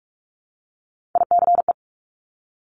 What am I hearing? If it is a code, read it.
Morse code, heard as a keyed tone, about 46 wpm, 700 Hz. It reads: ICE
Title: Multi-letter Phonemes Repeated 3x 45wpm